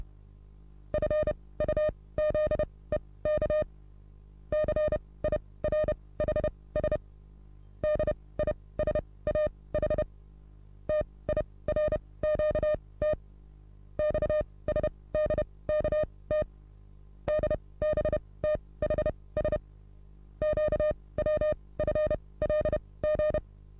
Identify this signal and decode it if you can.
Morse code, heard as a keyed tone, about 31 wpm, spaced as for 20 wpm, 610 Hz. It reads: FUZEK CIRHS DISAH TIRQT XSDKT DBTHS QWFLG
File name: repos / Morsekurs / DJ4UF Morsen27